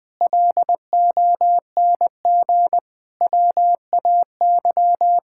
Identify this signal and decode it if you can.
Morse code, heard as a keyed tone, about 20 wpm, 700 Hz. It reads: LONG WAY
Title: Sentences with 7-Word Phrases - 20wpm